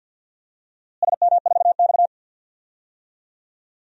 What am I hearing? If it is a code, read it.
Morse code, heard as a keyed tone, about 47 wpm, 700 Hz. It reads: SM4X
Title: DXCC-100 Call Signs - 4 Characters - Repeated 3x - 50wpm